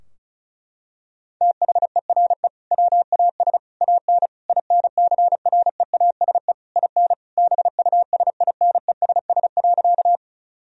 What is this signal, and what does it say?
Morse code, heard as a keyed tone, about 36 wpm, 700 Hz. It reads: THERE WAS AN INCREASE IN BUSINESS.
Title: Sentences from Top 1000 Words - Review Part 4 - 35wpm